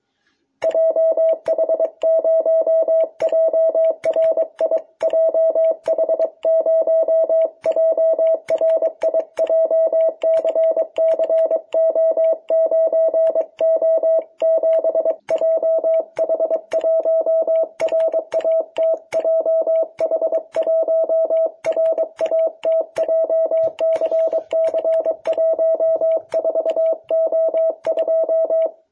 Morse code, heard as a keyed tone, about 23 wpm, 660 Hz. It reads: JH0JRI JH0JRI JCCO9O7 JH1RAT JH1RAT JCC14O2
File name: jh8cbh / 2021cw